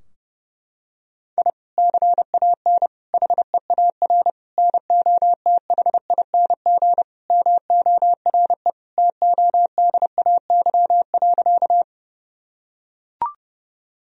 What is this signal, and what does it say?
Morse code, heard as a keyed tone, about 30 wpm, 700 Hz. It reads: I CAN HEAR NOTHING MORE TODAY.
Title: Sentences from Top 1000 Words - Review Part 4 - 30wpm